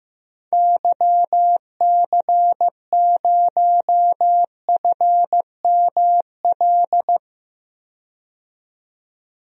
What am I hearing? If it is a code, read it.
Morse code, heard as a keyed tone, about 15 wpm, 700 Hz. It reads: YC0FML
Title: DXCC-100 Call Signs - 6 Characters - 15wpm